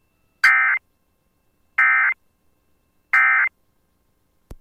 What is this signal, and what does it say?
Emergency Alert System (SAME) tones, heard as harsh buzzing bursts. It is heard from 0.4 to 3.5 s.